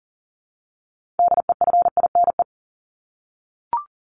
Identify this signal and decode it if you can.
Morse code, heard as a keyed tone, about 40 wpm, 700 Hz. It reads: DEFINE
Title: ED Suffix - Top 100 Words with Base Words 40wpm